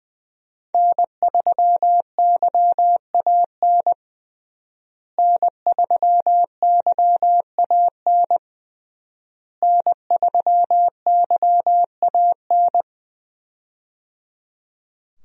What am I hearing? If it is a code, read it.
Morse code, heard as a keyed tone, about 20 wpm, 700 Hz. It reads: N3YAN N3YAN N3YAN
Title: USA Call Signs - 5 Characters - Repeated 3x - 20wpm